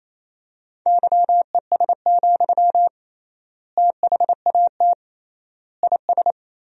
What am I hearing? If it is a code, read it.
Morse code, heard as a keyed tone, about 28 wpm, 700 Hz. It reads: YES, THAT IS
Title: 3x Word-Spacing - Sentences from Top 500 Words - 28wpm